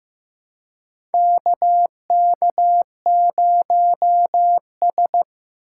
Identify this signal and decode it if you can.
Morse code, heard as a keyed tone, about 15 wpm, 700 Hz. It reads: KK0S
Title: USA Call Signs - 4 Characters - 2x Character Spacing - 15wpm